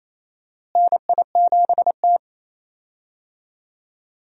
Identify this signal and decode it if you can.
Morse code, heard as a keyed tone, about 27 wpm, 700 Hz. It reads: NI7T
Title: USA Call Signs - 4 Characters - Repeated 3x - 28wpm